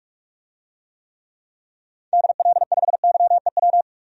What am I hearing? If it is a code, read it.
Morse code, heard as a keyed tone, about 46 wpm, 700 Hz. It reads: DL5YEW
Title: DXCC-100 Call Signs - 6 Characters - 45wpm